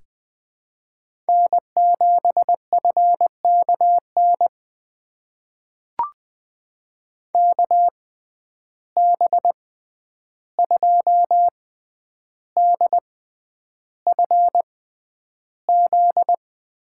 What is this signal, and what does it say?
Morse code, heard as a keyed tone, about 20 wpm, spaced as for 10 wpm, 700 Hz. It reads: N7FKN KB2DFZ
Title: USA Call Signs - 6x Character Spacing - 20wpm